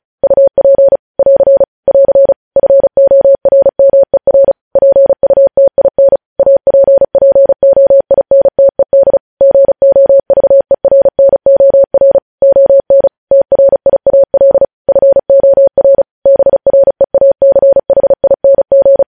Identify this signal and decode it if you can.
Morse code, heard as a keyed tone, about 36 wpm, 550 Hz. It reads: UP <AR> <AR> FORMER PUTIN APPOINTED GOVERNOR ON TRIAL FOR BREACHING
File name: podcast / News headlines-1x-35x35